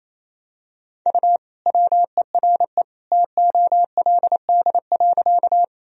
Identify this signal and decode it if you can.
Morse code, heard as a keyed tone, about 28 wpm, 700 Hz. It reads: U WERE TOLD.
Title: Sentences from Top 400 Words and Abbreviations 28wpm